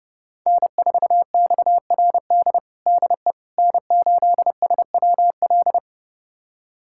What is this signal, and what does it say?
Morse code, heard as a keyed tone, about 31 wpm, 700 Hz. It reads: N4XRD DE N8SWL
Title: USA Callsign DE Callsign 30wpm